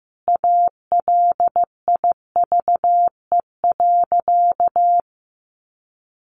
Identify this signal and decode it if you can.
Morse code, heard as a keyed tone, about 15 wpm, 700 Hz. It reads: ALIVE.
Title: A Sound Of Thunder 15wpm